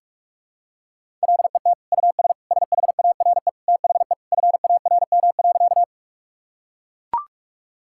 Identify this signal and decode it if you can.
Morse code, heard as a keyed tone, about 41 wpm, 700 Hz. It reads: LET US SHARE THE FARM.
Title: Sentences from Top 1000 Words - Review Part 6 - 45wpm